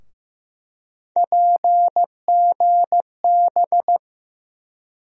Morse code, heard as a keyed tone, about 15 wpm, 700 Hz. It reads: PGB
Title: USA Call Sign Postfixes - 3 Letters 15wpm